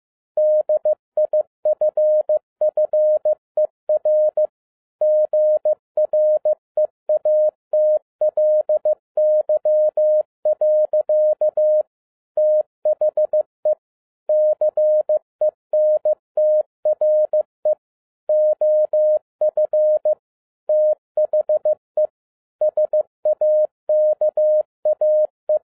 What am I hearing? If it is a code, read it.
Morse code, heard as a keyed tone, about 15 wpm, 610 Hz. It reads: DIFFER GREATLY. THE CENTRE OF THE SAKAE
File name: 2011-10-mp3 / n02-FIELD DAY CONTEST QRV FROM SAKAE VILLAGE 15wpm